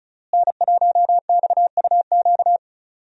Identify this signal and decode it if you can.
Morse code, heard as a keyed tone, about 34 wpm, 700 Hz. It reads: N1XUQ
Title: USA Call Signs - 5 Characters - Repeated 3x - 35wpm